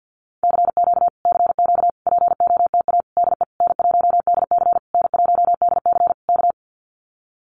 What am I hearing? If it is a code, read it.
Morse code, heard as a keyed tone, about 50 wpm, 700 Hz. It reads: CQ CQ POTA DE N1DC N1DC K
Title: Common QSO Phrases Encore 50wpm